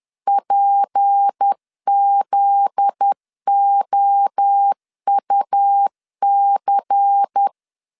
Morse code, heard as a keyed tone, about 10 wpm, 800 Hz. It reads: PZOUC
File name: LEZIONI / 10 SIGNS / 10B INT 10wpm 2844